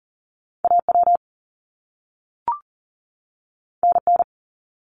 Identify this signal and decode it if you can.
Morse code, heard as a keyed tone, about 39 wpm, 700 Hz. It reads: AW NN
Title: Top Two Letter Combinations in Words 40wpm